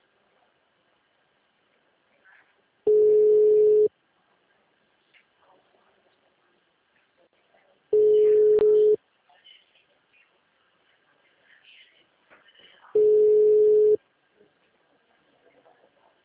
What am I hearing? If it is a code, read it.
Ringback tone (Europe, 425 Hz), with speech.